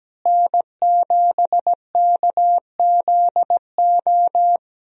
Morse code, heard as a keyed tone, about 17 wpm, 700 Hz. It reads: N7KZO
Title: USA Call Signs - 5 Characters - Repeated 3x - 17wpm